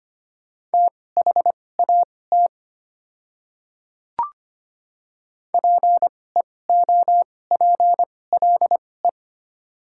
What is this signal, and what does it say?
Morse code, heard as a keyed tone, about 25 wpm, spaced as for 18 wpm, 700 Hz. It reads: THAT PEOPLE
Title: 2x Character Spacing - Top 100 Words - 25wpm